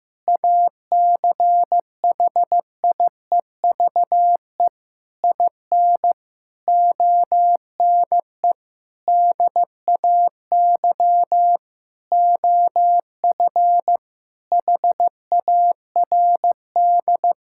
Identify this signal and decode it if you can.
Morse code, heard as a keyed tone, about 15 wpm, 700 Hz. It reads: ACHIEVE IN ONE DAY OF HARD
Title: Sentences from Top 1000 Words - Review Part 6 - 15wpm